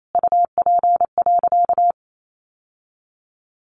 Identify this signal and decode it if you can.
Morse code, heard as a keyed tone, about 28 wpm, 700 Hz. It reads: UP.
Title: Sentences from Top 1000 Words 28wpm